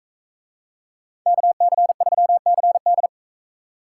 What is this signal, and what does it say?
Morse code, heard as a keyed tone, about 44 wpm, 700 Hz. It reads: KC3CD